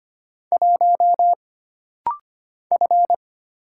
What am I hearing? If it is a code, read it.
Morse code, heard as a keyed tone, about 25 wpm, 700 Hz. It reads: 1 F
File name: Single Character (Letter:Number:Punctuation:Prosign) - Rapid-Fire - 25wpm